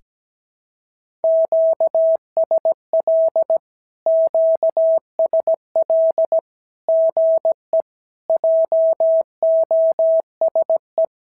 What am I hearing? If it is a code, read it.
Morse code, heard as a keyed tone, about 17 wpm, 650 Hz. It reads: QSL QSL GE JOSE
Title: SOTA QSO Practice 17wpm